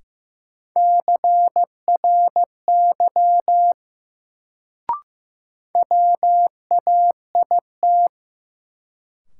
Morse code, heard as a keyed tone, about 15 wpm, 700 Hz. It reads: CRY WAIT
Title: Sets of 5 Words Encore - New Word Drill - 15wpm